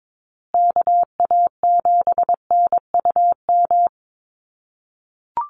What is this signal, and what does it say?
Morse code, heard as a keyed tone, about 22 wpm, 700 Hz. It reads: KA7NUM
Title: USA Call Signs Repeated 2x 22wpm